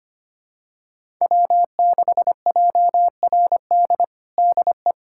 Morse code, heard as a keyed tone, about 25 wpm, 700 Hz. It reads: W6JRD DE
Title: USA Callsign DE Callsign 25wpm